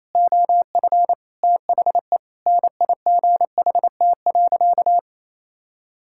Morse code, heard as a keyed tone, about 28 wpm, 700 Hz. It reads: OF THE NIGHT.
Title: Sentences from Top 1000 Words - Review Part 6 - 28wpm